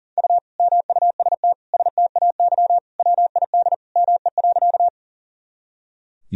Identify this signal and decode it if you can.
Morse code, heard as a keyed tone, about 40 wpm, 700 Hz. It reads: U MUST STAY WID ME.
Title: Sentences from Top 500 Words and Abbreviations 40wpm